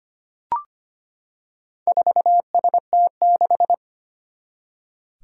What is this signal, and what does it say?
Morse code, heard as a keyed tone, about 25 wpm, 700 Hz. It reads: 4ST6
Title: Four Challenging Letter-Number Combinations 25wpm